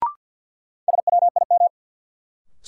Morse code, heard as a keyed tone, about 51 wpm, 700 Hz. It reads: SWIM
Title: Sentences from Top 1000 Words - Review Part 6 - New Word Drill - 50wpm